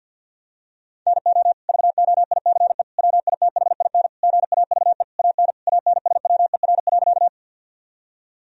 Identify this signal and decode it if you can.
Morse code, heard as a keyed tone, about 50 wpm, 700 Hz. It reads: NO VOICE WITHIN GAVE AN ANSWER.
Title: Sentences from Top 1000 Words - Review Part 4 - 50wpm